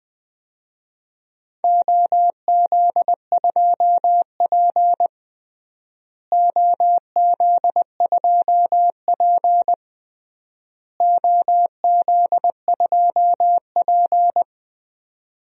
Morse code, heard as a keyed tone, about 20 wpm, 700 Hz. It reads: OZ2P OZ2P OZ2P
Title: DXCC-100 Call Signs - 4 Characters - Repeated 3x - 20wpm